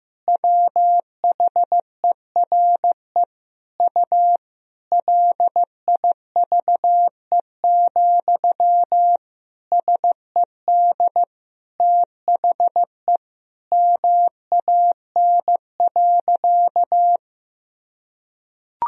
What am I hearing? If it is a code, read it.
Morse code, heard as a keyed tone, about 15 wpm, 700 Hz. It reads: WHERE U LIVE, SED THE MAN.
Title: Sentences from Top 200 Words and Abbreviations 15wpm